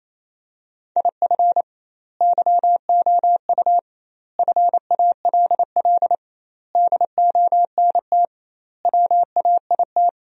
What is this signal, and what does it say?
Morse code, heard as a keyed tone, about 27 wpm, 700 Hz. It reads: IF YOU FALL DONT WAIT
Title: 2x Word-Spacing - Sentences from Top 500 Words - 28wpm